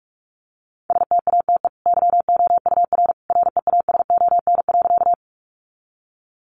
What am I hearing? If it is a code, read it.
Morse code, heard as a keyed tone, about 44 wpm, 700 Hz. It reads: STATE YOUR REASON.
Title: Sentences from Top 700 Words 45wpm